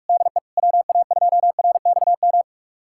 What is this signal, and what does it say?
Morse code, heard as a keyed tone, about 46 wpm, 700 Hz. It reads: DE WA2RXM
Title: USA Callsign DE Callsign 45wpm